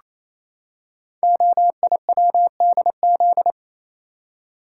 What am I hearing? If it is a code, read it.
Morse code, heard as a keyed tone, about 28 wpm, 700 Hz. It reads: OIWDZ
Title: Code Groups - 5 Letters - 28wpm